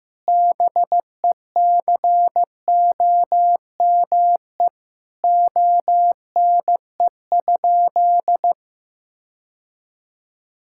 Morse code, heard as a keyed tone, about 15 wpm, 700 Hz. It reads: BECOME ONE?
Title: Sentences from Top 1600 Words 15wpm